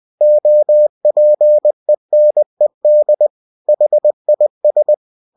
Morse code, heard as a keyed tone, about 20 wpm, 600 Hz. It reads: OPENED HIS